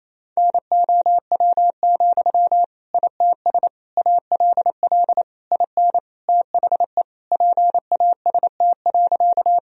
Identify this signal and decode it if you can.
Morse code, heard as a keyed tone, about 28 wpm, 700 Hz. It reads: NOW, ITS ALL IN THE PAST.
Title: Sentences from Top 1000 Words - Review Part 3 - 28wpm